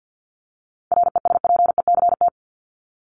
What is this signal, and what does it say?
Morse code, heard as a keyed tone, about 50 wpm, 700 Hz. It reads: RESPECT
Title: Top 1501-1750 Words 50wpm